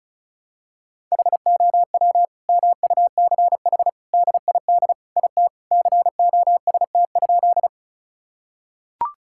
Morse code, heard as a keyed tone, about 36 wpm, 700 Hz. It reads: HOW MUCH DID IT COST?
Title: Sentences from Top 1000 Words - Review Part 5 - 35wpm